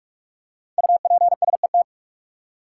Morse code, heard as a keyed tone, about 46 wpm, 700 Hz. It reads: UPSET
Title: Five Letter Words - 45wpm